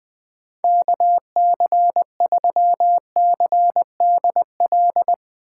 Morse code, heard as a keyed tone, about 20 wpm, 700 Hz. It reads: KC3CDL
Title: USA Call Signs - 3x Character Spacing - 20wpm